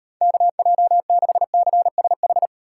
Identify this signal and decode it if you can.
Morse code, heard as a keyed tone, about 32 wpm, 700 Hz. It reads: KJ6CSH